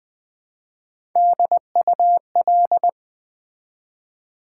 Morse code, heard as a keyed tone, about 20 wpm, 700 Hz. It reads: DUL
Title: Build A Word - 1 Letter At A Time - 4-Letter Words - 20wpm